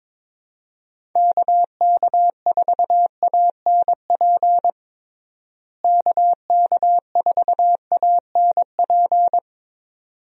Morse code, heard as a keyed tone, about 21 wpm, 700 Hz. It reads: KK4ANP KK4ANP
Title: USA Call Signs - 6 Characters - Repeated 2x - 22wpm